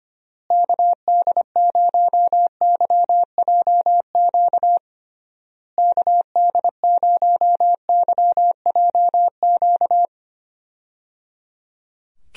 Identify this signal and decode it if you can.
Morse code, heard as a keyed tone, about 25 wpm, 700 Hz. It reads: KD0YJQ KD0YJQ
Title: USA Call Signs - 6 Characters - Repeated 3x - 25wpm